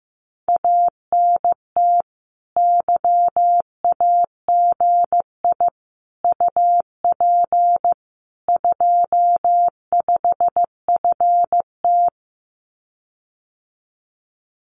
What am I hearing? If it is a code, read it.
Morse code, heard as a keyed tone, about 15 wpm, 700 Hz. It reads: ANT YAGI UP 25FT
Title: Common QSO Phrases Encore 15wpm